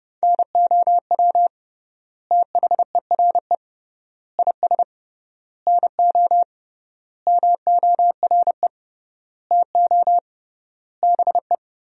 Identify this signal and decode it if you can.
Morse code, heard as a keyed tone, about 30 wpm, 700 Hz. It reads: NOW THERE IS NO MORE TO BE
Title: 3x Word-Spacing - Sentences from Top 100 Words - 30wpm